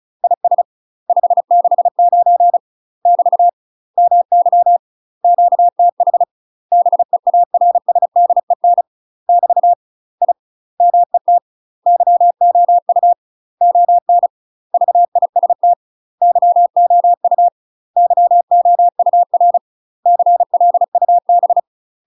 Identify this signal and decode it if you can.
Morse code, heard as a keyed tone, about 34 wpm, 700 Hz. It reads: IS 569 <BT> MY QTH BEARSDEN <BT> I MET YOU ON VIST YOU YOUR CLUB
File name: UK MASTER QSOs TRAINING / 35WPM / UK MASTER QSOs 0009